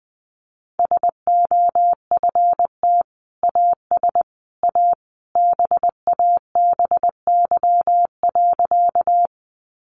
Morse code, heard as a keyed tone, about 20 wpm, 700 Hz. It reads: SOFT AS A BABY.
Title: Sentences from Top 1500 Words 20wpm